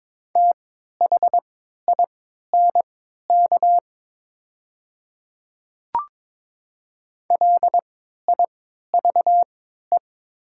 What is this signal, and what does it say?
Morse code, heard as a keyed tone, about 22 wpm, spaced as for 13 wpm, 700 Hz. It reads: THINK LIVE
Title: 3x Character Spacing - Top 200 Words - 22wpm